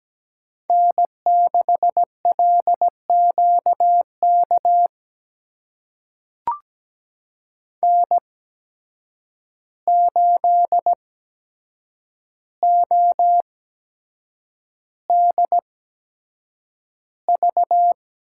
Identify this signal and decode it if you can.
Morse code, heard as a keyed tone, about 17 wpm, spaced as for 7 wpm, 700 Hz. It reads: N6LQK N8ODV
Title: USA Call Signs - 8x Character Spacing - 17wpm